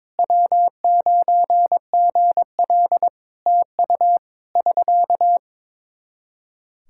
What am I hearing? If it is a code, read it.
Morse code, heard as a keyed tone, about 21 wpm, 700 Hz. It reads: W9GL TU <SK>
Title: K1USN SST QSO Practice 22wpm